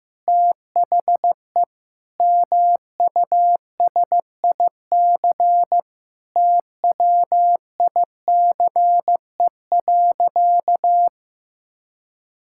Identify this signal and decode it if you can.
Morse code, heard as a keyed tone, about 15 wpm, 700 Hz. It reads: THE MUSIC TWICE.
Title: Sentences from Top 1600 Words 15wpm